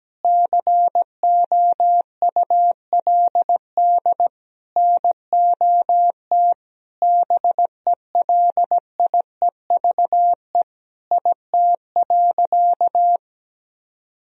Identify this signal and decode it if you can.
Morse code, heard as a keyed tone, about 17 wpm, 700 Hz. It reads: COULD NOT BELIEVE IT.
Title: Sentences from Top 1000 Words - Review Part 3 - 17wpm